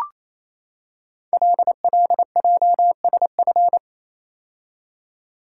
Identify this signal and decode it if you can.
Morse code, heard as a keyed tone, about 27 wpm, 700 Hz. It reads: LLJSF
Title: Code Groups - 5 Letters - 28wpm